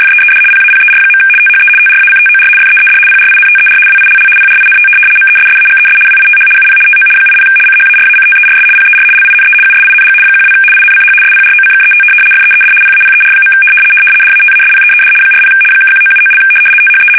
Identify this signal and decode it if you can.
RTTY, about 75 baud. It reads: FEED PAPER. THE TRACTOR FEED PRINTER WILL PRINT MESSAGES WITH UP TO 80 CHARACTERS PER LINE USING 12 INCH SPROCKET FEED PAPER. DATASPEED 40 (1979) THE
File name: fsk850-75B / teletype-corp